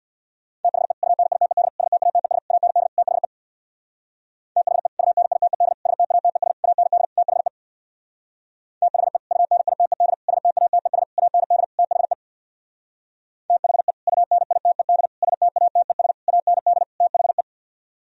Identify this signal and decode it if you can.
Morse code, heard as a keyed tone, about 64 wpm, 700 Hz. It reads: THE UNITED STATES AND THE THE UNITED STATES AND THE THE UNITED STATES AND THE THE UNITED STATES AND THE